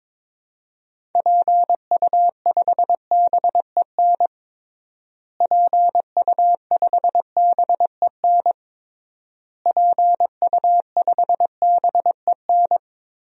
Morse code, heard as a keyed tone, about 21 wpm, 700 Hz. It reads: PU5BEN PU5BEN PU5BEN
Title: DXCC-100 Call Signs - 6 Characters - Repeated 3x - 22wpm